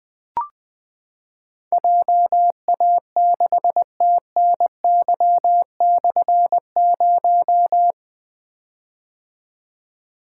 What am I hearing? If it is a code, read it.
Morse code, heard as a keyed tone, about 21 wpm, 700 Hz. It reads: JA6TNY/0
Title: Difficult Call Signs 20wpm